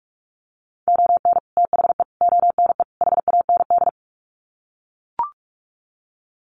Morse code, heard as a keyed tone, about 44 wpm, 700 Hz. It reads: ON THE ONE HAND
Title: Sets of 4 Words 45wpm